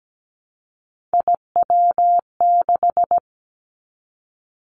Morse code, heard as a keyed tone, about 17 wpm, 700 Hz. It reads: IW6
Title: Three Challenging Letter-Number Combinations 17wpm